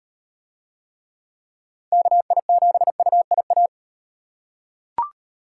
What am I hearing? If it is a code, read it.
Morse code, heard as a keyed tone, about 36 wpm, 700 Hz. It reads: KI7UIA